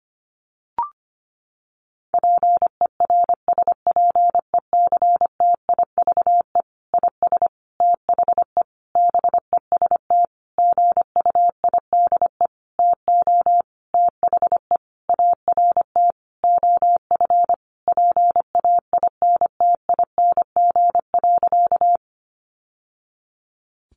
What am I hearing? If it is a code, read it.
Morse code, heard as a keyed tone, about 25 wpm, 700 Hz. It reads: PERSPECTIVE IS THE BEST GUIDE TO THE ART OF PAINTING.
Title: Sentences from Top 1200 Words 25wpm